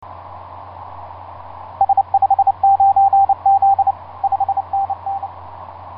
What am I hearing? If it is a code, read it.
Morse code, heard as a keyed tone, about 30 wpm, 790 Hz, with radio noise. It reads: S59Z 5NN
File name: s79gm s59z etc 10m cw 30mar2010 1126